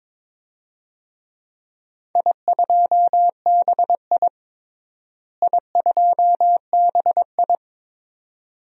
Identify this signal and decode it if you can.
Morse code, heard as a keyed tone, about 22 wpm, 700 Hz. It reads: I2BI I2BI
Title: DXCC-100 Call Signs - 4 Characters - Repeated 2x - 22wpm